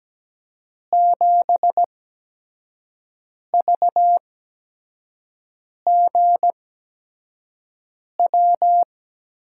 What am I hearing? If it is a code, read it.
Morse code, heard as a keyed tone, about 17 wpm, spaced as for 5 wpm, 700 Hz. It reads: 7VGW
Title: USA Call Signs - 8x Character Spacing - 17wpm